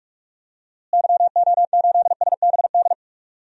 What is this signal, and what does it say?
Morse code, heard as a keyed tone, about 48 wpm, 700 Hz. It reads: YO8SBD
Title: DXCC-100 Call Signs - 6 Characters - Repeated 3x - 45wpm